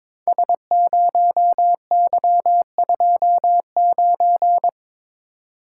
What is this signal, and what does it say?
Morse code, heard as a keyed tone, about 22 wpm, 690 Hz. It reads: S0Y29
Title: Code Groups - 5 Letter:Numbers - 22wpm